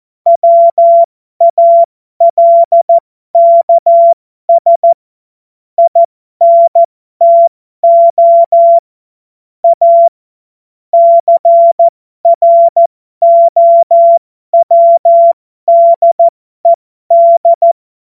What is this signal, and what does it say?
Morse code, heard as a keyed tone, about 14 wpm, spaced as for 12 wpm, 680 Hz. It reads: WALKS INTO A CROWDED
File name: Morsekurs / Witze